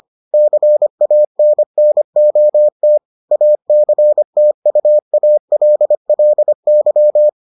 Morse code, heard as a keyed tone, about 25 wpm, 600 Hz. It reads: CANNOT ACTUALLY